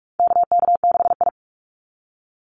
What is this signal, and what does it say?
Morse code, heard as a keyed tone, about 44 wpm, 700 Hz. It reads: KK6I
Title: Call Signs Repeated 2x - 45wpm